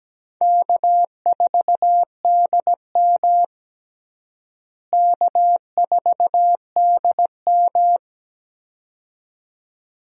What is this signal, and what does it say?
Morse code, heard as a keyed tone, about 17 wpm, 700 Hz. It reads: K4DM K4DM
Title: USA Call Signs - 4 Characters - Repeated 2x - 17wpm